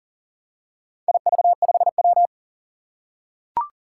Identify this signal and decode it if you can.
Morse code, heard as a keyed tone, about 38 wpm, 700 Hz. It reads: IV5W
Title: Four Challenging Letter-Number Combinations 40wpm